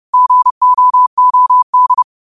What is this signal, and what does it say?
Morse code, heard as a keyed tone, about 30 wpm, 1000 Hz. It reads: GOOD
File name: qotdfiles / 30wpm